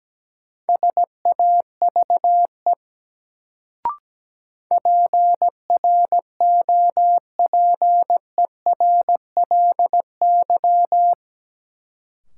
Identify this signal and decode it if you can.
Morse code, heard as a keyed tone, about 17 wpm, 700 Hz. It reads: SAVE PROPERLY
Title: Sentences from Top 1100 Words - New Word Drill - 17wpm